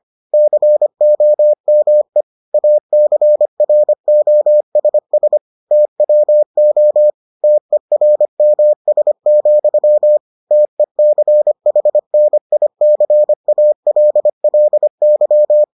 Morse code, heard as a keyed tone, about 25 wpm, 600 Hz. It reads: COME ACROSS TWO TERMS, TECHNICALLY